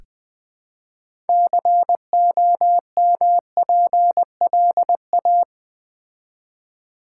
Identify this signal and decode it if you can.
Morse code, heard as a keyed tone, about 20 wpm, 700 Hz. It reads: COMPLA
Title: Build A Word - 3 Letters At A Time - 8-Letter Words - 20wpm